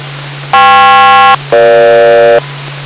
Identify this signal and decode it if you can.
Aircraft SELCAL, heard as two pairs of tones. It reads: KP-EG